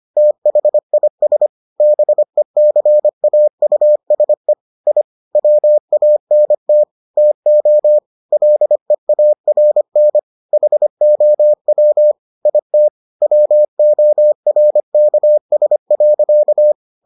Morse code, heard as a keyed tone, about 25 wpm, 600 Hz. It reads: THIS BECAUSE I WANT TO LEARN HOW IT WORKS.